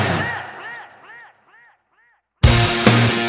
There is no code.